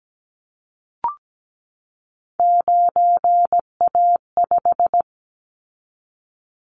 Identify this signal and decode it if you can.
Morse code, heard as a keyed tone, about 17 wpm, 700 Hz. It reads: 9A5
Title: Callsign Prefixes for Top 25 DXCC - 17wpm